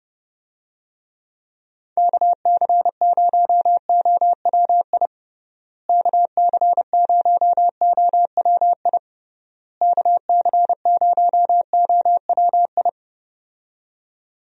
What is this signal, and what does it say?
Morse code, heard as a keyed tone, about 30 wpm, 700 Hz. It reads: KC0OWI KC0OWI KC0OWI
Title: USA Call Signs - 6 Characters - Repeated 3x - 30wpm